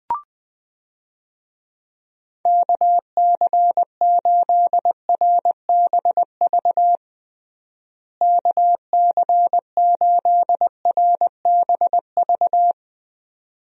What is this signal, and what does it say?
Morse code, heard as a keyed tone, about 21 wpm, 700 Hz. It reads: KC8RBV KC8RBV
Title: USA Call Signs - 6 Characters - Repeated 2x - 20wpm